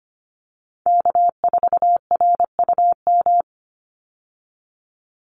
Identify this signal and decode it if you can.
Morse code, heard as a keyed tone, about 25 wpm, 700 Hz. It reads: K4RUM
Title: Call Signs Repeated 3x - 25wpm